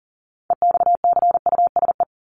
Morse code, heard as a keyed tone, about 40 wpm, 700 Hz. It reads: EXCUSE
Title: Top 501-750 Words 40wpm